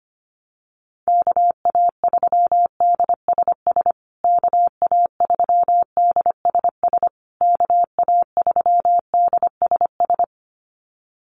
Morse code, heard as a keyed tone, about 25 wpm, 700 Hz. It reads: KA3DSS KA3DSS KA3DSS
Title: USA Call Signs Repeated 3x 25wpm